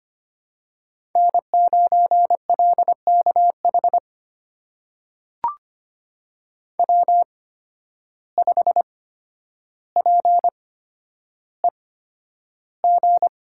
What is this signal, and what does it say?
Morse code, heard as a keyed tone, about 25 wpm, spaced as for 11 wpm, 700 Hz. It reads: N9LKH W5PEG
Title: USA Call Signs - 5 Characters - 8x Character Spacing - 25wpm